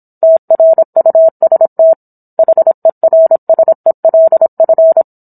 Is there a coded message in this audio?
Yes: Morse code from 0.2 to 5.0 s, heard as a keyed tone, about 27 wpm, 650 Hz. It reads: TRUST HERSELF